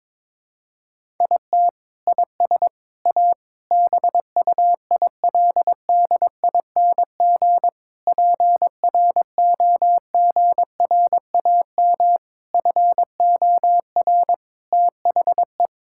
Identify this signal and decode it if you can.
Morse code, heard as a keyed tone, about 22 wpm, 700 Hz. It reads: IT IS A BUILDING PROGRAM FOR THE
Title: Sentences from Top 1000 Words - Review Part 5 - 22wpm